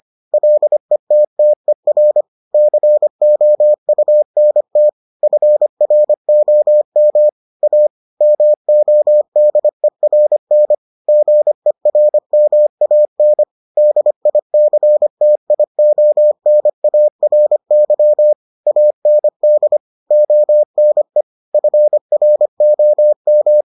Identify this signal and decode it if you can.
Morse code, heard as a keyed tone, about 25 wpm, 600 Hz. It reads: LETTER COUNT FROM A MODERN GERMAN DICTIONARY AND ONE FROM